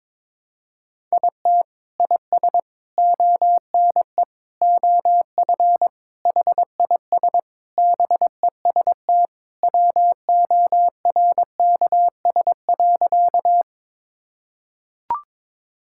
Morse code, heard as a keyed tone, about 22 wpm, 700 Hz. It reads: IT IS ONE OF HIS BEST WORKS.
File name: practice / 22wpm / Sentences with 3-Word Phrases - 22wpm